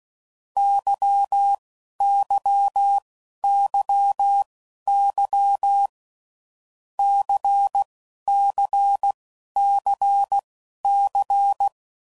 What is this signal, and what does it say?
Morse code, heard as a keyed tone, about 16 wpm, spaced as for 11 wpm, 800 Hz. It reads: YYYY CCCC
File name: Groups / Basic Letter Groups / LFCYQ